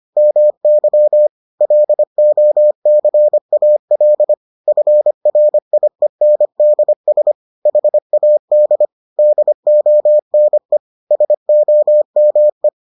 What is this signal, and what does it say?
Morse code, heard as a keyed tone, about 25 wpm, 600 Hz. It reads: MY LOCAL FRIENDS HAD DONE SOME